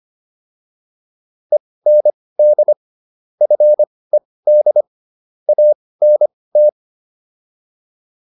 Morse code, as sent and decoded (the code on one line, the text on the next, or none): . -. -.. / ..-. -.. / .- -. -
END FED ANT